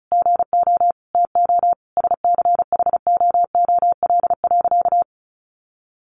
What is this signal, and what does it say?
Morse code, heard as a keyed tone, about 34 wpm, 700 Hz. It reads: GO TO SCHOOL.
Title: Easy Phrases 35wpm